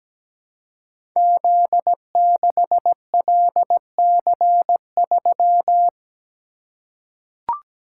Morse code, heard as a keyed tone, about 17 wpm, 700 Hz. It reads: Z6LC3
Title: Code Groups - 5 Letter:Numbers - 17wpm